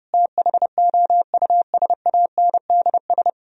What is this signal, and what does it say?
Morse code, heard as a keyed tone, about 37 wpm, 700 Hz. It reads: THOUSANDS
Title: Speed-Racing - Sets of 4 Words 30wpm